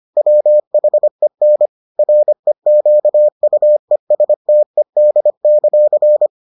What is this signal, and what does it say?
Morse code, heard as a keyed tone, about 25 wpm, 600 Hz. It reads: WHEN REQUESTED;